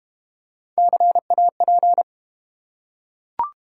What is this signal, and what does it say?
Morse code, heard as a keyed tone, about 32 wpm, 700 Hz. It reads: CAP